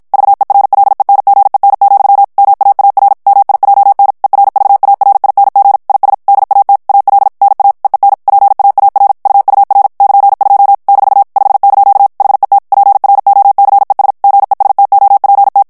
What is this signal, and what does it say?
Morse code, heard as a keyed tone, about 53 wpm, 800 Hz. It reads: VERLENGEN, MAAR MIJN ERVARING IS DAT AL NA EEN PAAR UUR <KN>2 - 5) HET PROCES GESTOPT
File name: HOLLANDAIS texte clair 2 vitesse 50 mots minute